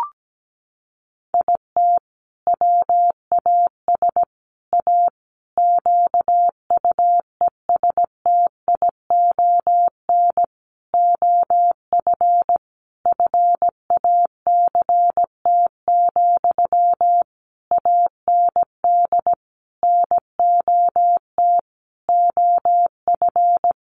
Morse code, heard as a keyed tone, about 17 wpm, 700 Hz. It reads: IT WAS A QUESTION OF FACT, AND NOT OF
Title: Sentences from Top 1400 Words 17wpm